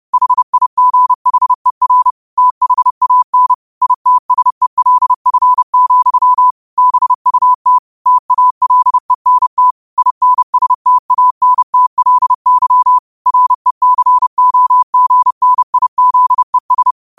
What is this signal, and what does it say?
Morse code, heard as a keyed tone, about 30 wpm, 1000 Hz. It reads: HIGHER THAN ITSELF, BUT TALENT INSTANTLY RECOGNIZES
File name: 30wpm